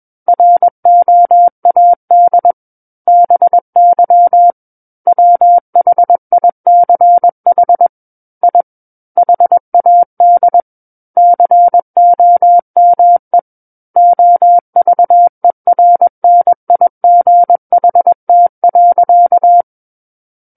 Morse code, heard as a keyed tone, about 21 wpm, 700 Hz. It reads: ROAD BY WHICH I HAD COME OVERNIGHT.
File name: Book One / 12-Chapter 12 - 21 WPM